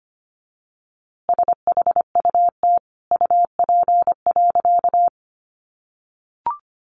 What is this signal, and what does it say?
Morse code, heard as a keyed tone, about 25 wpm, 700 Hz. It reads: SHUT UP.
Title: A Sound Of Thunder 25wpm